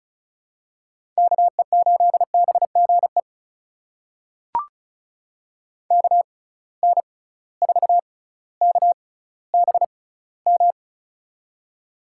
Morse code, heard as a keyed tone, about 36 wpm, spaced as for 18 wpm, 700 Hz. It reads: KE8BGE KN4KBM
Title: USA Call Signs - 6 Characters - 6x Character Spacing - 35wpm